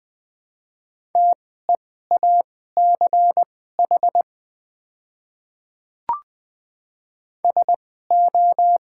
Morse code, heard as a keyed tone, about 20 wpm, spaced as for 14 wpm, 690 Hz. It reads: TEACH SO
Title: 2x Character Spacing - Top 500 Words - 20wpm